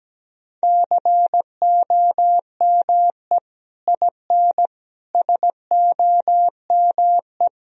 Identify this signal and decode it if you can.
Morse code, heard as a keyed tone, about 17 wpm, 700 Hz. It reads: COME IN SOME
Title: Sentences from Top 1000 Words - Review Part 2 - 17wpm